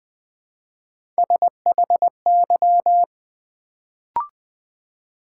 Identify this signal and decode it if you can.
Morse code, heard as a keyed tone, about 20 wpm, 700 Hz. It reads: SHY
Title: Build A Word - 1 Letter At A Time - 3-Letter Words - 20wpm